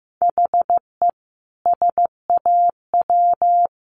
Morse code, heard as a keyed tone, about 15 wpm, 700 Hz. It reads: HE SAW
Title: Sentences from Top 500 Words 15wpm